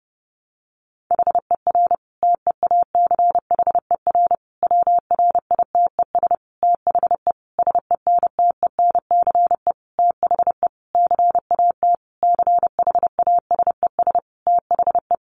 Morse code, heard as a keyed tone, about 30 wpm, 700 Hz. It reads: HER TEACHER WRITES THE SENTENCE THE CAT CHASES THE
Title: Additional Easy Phrases 30wpm